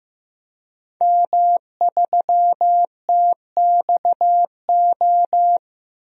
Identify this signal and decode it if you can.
Morse code, heard as a keyed tone, about 15 wpm, 700 Hz. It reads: M3TXO
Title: DXCC-100 Call Signs - 5 Characters - Repeated 2x - 15wpm